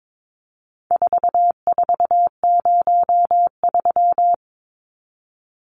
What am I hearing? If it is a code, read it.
Morse code, heard as a keyed tone, about 22 wpm, 700 Hz. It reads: 4403
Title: Numbers - 4-Digit 22wpm